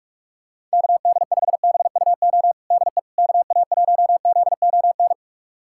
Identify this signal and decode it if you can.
Morse code, heard as a keyed tone, about 46 wpm, 690 Hz. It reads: KD5BUO DE KA1ZON
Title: USA Callsign DE Callsign - 6 Characters - 45wpm